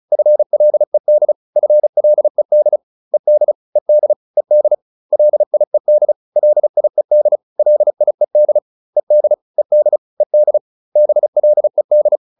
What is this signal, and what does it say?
Morse code, heard as a keyed tone, about 34 wpm, 600 Hz. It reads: FLED FLED ED ED ED LIED LIED LIED ED ED ED BLED